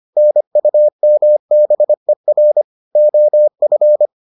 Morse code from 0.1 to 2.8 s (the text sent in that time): NUMBER